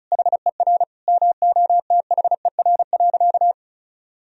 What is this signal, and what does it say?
Morse code, heard as a keyed tone, about 34 wpm, 700 Hz. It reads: HER MOTHER.
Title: Sentences from Top 1000 Words - Review Part 2 - 35wpm